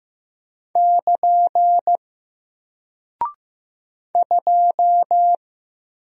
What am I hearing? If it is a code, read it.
Morse code, heard as a keyed tone, about 15 wpm, 700 Hz. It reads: <KN> 2
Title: Single Character (Letter:Number:Punctuation:Prosign) 15wpm